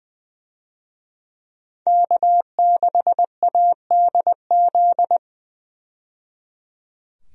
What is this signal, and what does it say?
Morse code, heard as a keyed tone, about 20 wpm, 700 Hz. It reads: K6ADZ
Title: USA Call Signs - 5 Characters - 20wpm